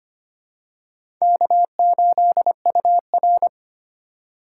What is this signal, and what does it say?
Morse code, heard as a keyed tone, about 25 wpm, 700 Hz. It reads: K8UR
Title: USA Call Signs - 8x Character Spacing - 25wpm